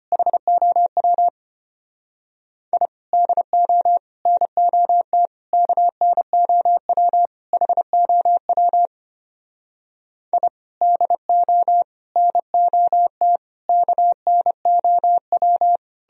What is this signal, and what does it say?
Morse code, heard as a keyed tone, about 32 wpm, 700 Hz. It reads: HOW I DO NOT KNOW HOW I DO NOT KNOW